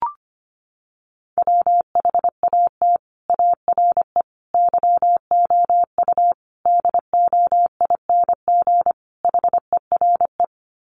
Morse code, heard as a keyed tone, about 25 wpm, 700 Hz. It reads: WHAT ARE YOU DOING HERE
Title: Sets of 5 Words 25wpm